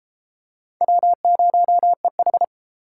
Morse code, heard as a keyed tone, about 31 wpm, 700 Hz. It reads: W0EH